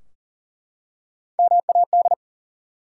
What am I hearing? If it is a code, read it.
Morse code, heard as a keyed tone, about 40 wpm, 700 Hz. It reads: MAD
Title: Three Letter Words - 40wpm